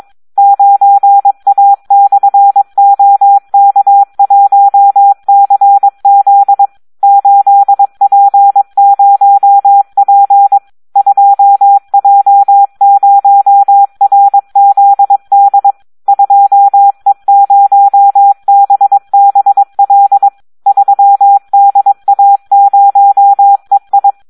Morse code, heard as a keyed tone, about 22 wpm, 800 Hz, with radio noise. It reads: 9A/OK1CZ 8P0P 2J0RZD 2E0BBL 3DA0EI
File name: Audio practica CW / SD 1 22